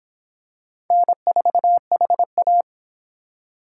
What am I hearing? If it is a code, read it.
Morse code, heard as a keyed tone, about 25 wpm, 700 Hz. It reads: N4HA